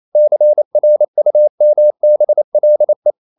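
Morse code, heard as a keyed tone, about 28 wpm, 600 Hz. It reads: CRUMBLE
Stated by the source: Morse code practice at 28 wpm, text of Ken Follett's novel Never in English, from CW med Gnister.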